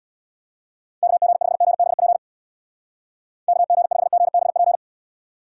Morse code, heard as a keyed tone, about 65 wpm, 700 Hz. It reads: KG5GBP KG5GBP